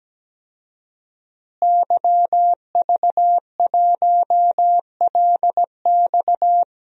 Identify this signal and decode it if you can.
Morse code, heard as a keyed tone, about 17 wpm, 700 Hz. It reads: YV1LX
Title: DXCC-100 Call Signs - 5 Characters - Repeated 2x - 17wpm